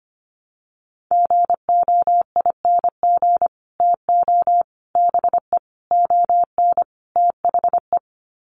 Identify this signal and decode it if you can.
Morse code, heard as a keyed tone, about 25 wpm, 700 Hz. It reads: GOING TO BE ON THE
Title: Sets of 5 Words Encore2 25wpm